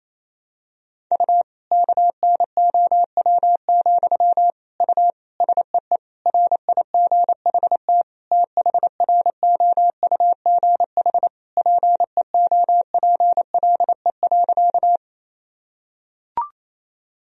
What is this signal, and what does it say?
Morse code, heard as a keyed tone, about 28 wpm, 700 Hz. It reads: U KNOW, U SEE RIGHT THROUGH PEOPLE.
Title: Sentences from Top 200 Words and Abbreviations 28wpm